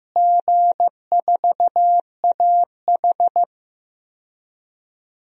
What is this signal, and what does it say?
Morse code, heard as a keyed tone, about 15 wpm, 700 Hz. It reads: G4AH
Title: DXCC-100 Call Signs - 4 Characters - Repeated 3x - 15wpm